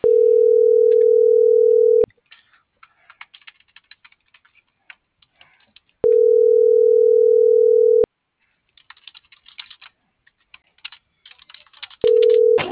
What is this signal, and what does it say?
Ringback tone (North America), 440 + 480 Hz.